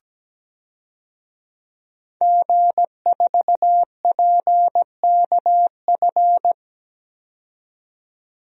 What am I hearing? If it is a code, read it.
Morse code, heard as a keyed tone, about 17 wpm, 700 Hz. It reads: G4PKF
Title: DXCC-100 Call Signs - 5 Characters - 17wpm